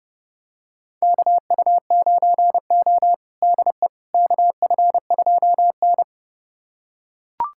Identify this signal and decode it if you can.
Morse code, heard as a keyed tone, about 30 wpm, 700 Hz. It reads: KU9O DE KF2N
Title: USA Callsign DE Callsign - 4 Characters - 30wpm